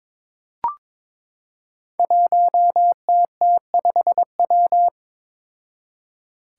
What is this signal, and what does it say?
Morse code, heard as a keyed tone, about 22 wpm, 700 Hz. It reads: 1TT5W
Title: Five Challenging Letter-Number Combinations 22wpm